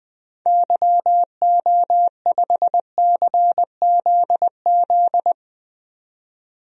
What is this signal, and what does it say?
Morse code, heard as a keyed tone, about 20 wpm, 700 Hz. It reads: YO5CZZ
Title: DXCC-100 Call Signs - 6 Characters - 20wpm